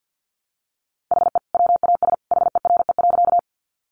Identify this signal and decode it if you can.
Morse code, heard as a keyed tone, about 50 wpm, 700 Hz. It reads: HE WAS HERE.
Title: Sentences from Top 200 Words 50wpm